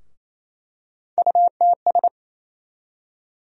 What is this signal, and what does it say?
Morse code, heard as a keyed tone, about 28 wpm, 700 Hz. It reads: UTS
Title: USA Call Sign Postfixes - 3 Letters 28wpm